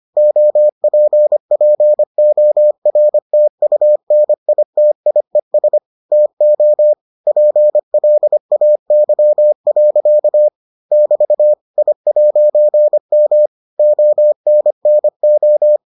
Morse code, heard as a keyed tone, about 25 wpm, 600 Hz. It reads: OPPORTUNITIES TO PLAY. <BT> I'M ONNO